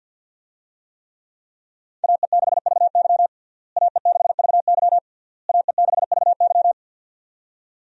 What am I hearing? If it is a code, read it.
Morse code, heard as a keyed tone, about 47 wpm, 700 Hz. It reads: AE6VY AE6VY AE6VY
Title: USA Call Signs - 5 Characters - Repeated 3x - 50wpm